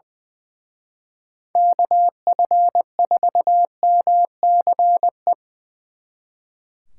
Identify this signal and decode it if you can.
Morse code, heard as a keyed tone, about 23 wpm, 700 Hz. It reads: KF4MCE
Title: Speed Racing - USA 2x3 Call Signs 20wpm